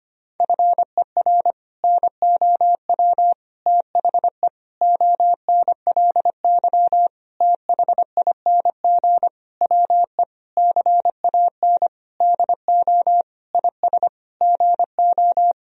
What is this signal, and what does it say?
Morse code, heard as a keyed tone, about 25 wpm, 700 Hz. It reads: FER NOW THE ONLY THING WE CAN DO IS GO
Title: Sentences from Top 200 Words and Abbreviations 25wpm